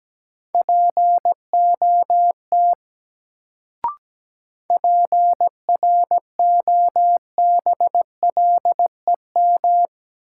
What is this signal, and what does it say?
Morse code, heard as a keyed tone, about 17 wpm, 700 Hz. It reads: POT PROBLEM
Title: Sentences from Top 1000 Words - New Word Drill - 17wpm